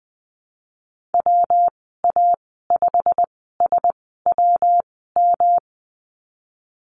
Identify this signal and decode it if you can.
Morse code, heard as a keyed tone, about 19 wpm, spaced as for 14 wpm, 700 Hz. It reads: WA5SWM
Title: 238 Lesson 38g - Letter q - Call Signs 20wpm